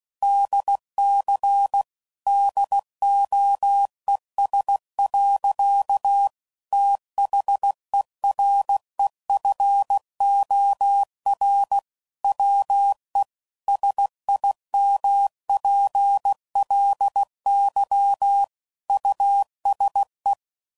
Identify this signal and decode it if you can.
Morse code, heard as a keyed tone, about 16 wpm, 800 Hz. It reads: DC DOES. THEREFOR WE SIMPLY USE
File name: CW Code Groups @15WPM/Technical Subjects